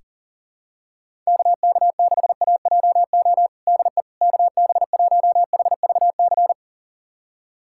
Morse code, heard as a keyed tone, about 40 wpm, 700 Hz. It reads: KK6AJO DE KB1HVC
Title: USA Callsign DE Callsign 40wpm